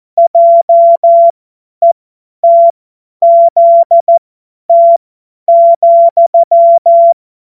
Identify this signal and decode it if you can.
Morse code, heard as a keyed tone, about 14 wpm, spaced as for 10 wpm, 680 Hz. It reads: JETZT,